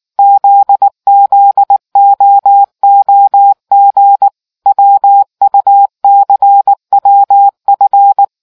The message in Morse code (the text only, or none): ZZOOG WUCWF